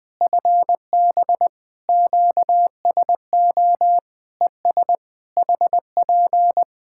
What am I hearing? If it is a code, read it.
Morse code, heard as a keyed tone, about 20 wpm, 700 Hz. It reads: FB QSO ES HP
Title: K1USN SST QSO Practice 20wpm